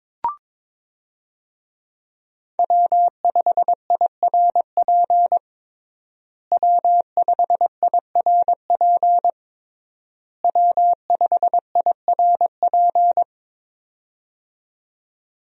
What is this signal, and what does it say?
Morse code, heard as a keyed tone, about 22 wpm, 700 Hz. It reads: W5IRP W5IRP W5IRP
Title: USA Call Signs - 5 Characters - Repeated 3x - 22wpm